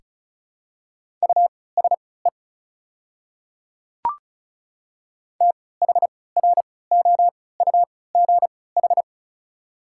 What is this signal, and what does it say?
Morse code, heard as a keyed tone, about 34 wpm, spaced as for 20 wpm, 700 Hz. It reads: USE THROUGH
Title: 3x Character Spacing - Top 500 Words - 35wpm